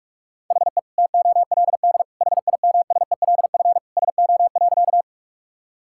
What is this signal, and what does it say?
Morse code, heard as a keyed tone, about 42 wpm, 700 Hz. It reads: HE TOLD HIMSELF SO.
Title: Sentences from Top 1600 Words 45wpm